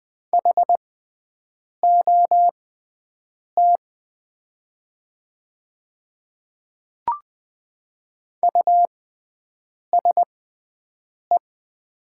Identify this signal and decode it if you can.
Morse code, heard as a keyed tone, about 20 wpm, spaced as for 7 wpm, 700 Hz. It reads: HOT USE
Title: 6x Character Spacing - Top 100 Words - 20wpm